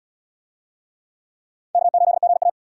Morse code, heard as a keyed tone, about 72 wpm, 700 Hz. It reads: K8GA